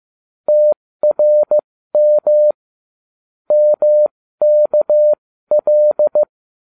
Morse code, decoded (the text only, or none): TRM MKL